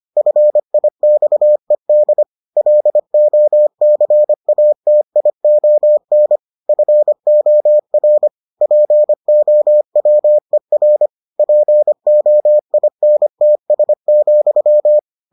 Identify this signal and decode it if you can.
Morse code, heard as a keyed tone, about 25 wpm, 600 Hz. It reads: FIXED LOCATION FOR POWER POINTS,